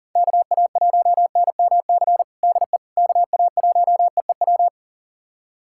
Morse code, heard as a keyed tone, about 40 wpm, 700 Hz. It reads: KA1NMC DE KA1EEW
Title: USA Callsign DE Callsign 40wpm